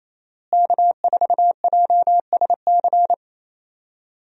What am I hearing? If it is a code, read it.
Morse code, heard as a keyed tone, about 28 wpm, 700 Hz. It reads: K4JSC
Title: USA Call Signs - 5 Characters - Repeated 3x - 28wpm